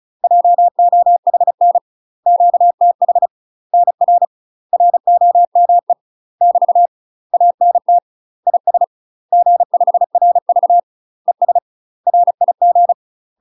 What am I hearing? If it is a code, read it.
Morse code, heard as a keyed tone, about 36 wpm, 700 Hz. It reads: JOHN QTH NR ROME <BT> ANT IS G5RV ES RIG
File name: UK MASTER QSOs TRAINING / 35WPM / UK MASTER QSOs 0031